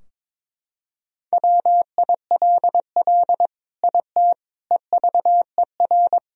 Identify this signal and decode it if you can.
Morse code, heard as a keyed tone, about 22 wpm, 700 Hz. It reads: WILL IT EVER
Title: Sentences from Top 1000 Words - Review Part 2 - 22wpm